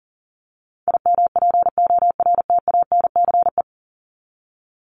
Morse code, heard as a keyed tone, about 40 wpm, 700 Hz. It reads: IMPORTANCE
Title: Top 1001-1250 Words 40wpm